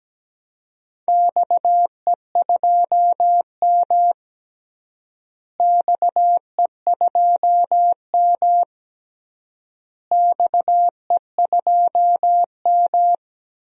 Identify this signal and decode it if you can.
Morse code, heard as a keyed tone, about 17 wpm, 700 Hz. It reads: XE2M XE2M XE2M
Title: DXCC-100 Call Signs - 4 Characters - Repeated 3x - 17wpm